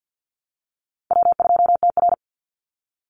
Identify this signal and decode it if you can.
Morse code, heard as a keyed tone, about 50 wpm, 700 Hz. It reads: W2TR
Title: Call Signs - 50wpm